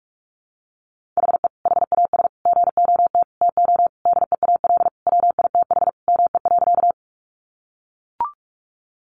Morse code, heard as a keyed tone, about 44 wpm, 700 Hz. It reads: HE HAS GOT TO DEAL WITH ME.
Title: Sentences from Top 800 Words 45wpm